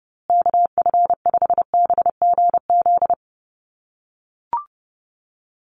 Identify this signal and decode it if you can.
Morse code, heard as a keyed tone, about 30 wpm, 700 Hz. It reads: KF5BGZ
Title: USA Call Signs Repeated 3x 30wpm